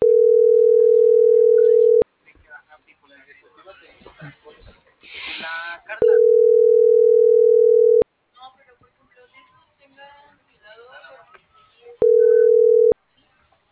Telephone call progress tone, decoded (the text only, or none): ringback tone